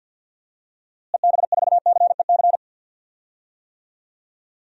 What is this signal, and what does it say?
Morse code, heard as a keyed tone, about 52 wpm, 700 Hz. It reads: EB4CEC
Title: DXCC-100 Call Signs - 6 Characters - Repeated 3x - 50wpm